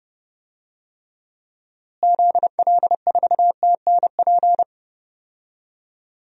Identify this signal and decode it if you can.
Morse code, heard as a keyed tone, about 30 wpm, 700 Hz. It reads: ZL4TNP
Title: DXCC-100 Call Signs - 6 Characters - Repeated 3x - 30wpm